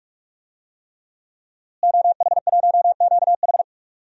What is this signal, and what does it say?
Morse code, heard as a keyed tone, about 45 wpm, 700 Hz. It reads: OH1QH
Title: DXCC-100 Call Signs - 5 Characters - Repeated 3x - 45wpm